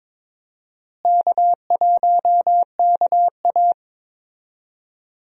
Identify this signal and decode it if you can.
Morse code, heard as a keyed tone, about 23 wpm, 700 Hz. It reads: K1KA
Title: USA Call Signs - 4 Characters - Repeated 2x - 22wpm